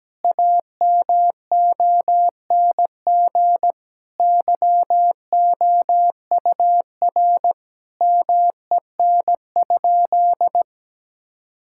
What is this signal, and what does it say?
Morse code, heard as a keyed tone, about 17 wpm, 700 Hz. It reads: AMONG YOUR MEN?
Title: Sentences from Top 1000 Words - Review Part 6 - 17wpm